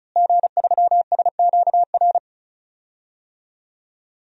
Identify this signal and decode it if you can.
Morse code, heard as a keyed tone, about 34 wpm, 700 Hz. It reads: G3SQR
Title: DXCC-100 Call Signs - 5 Characters - Repeated 2x - 35wpm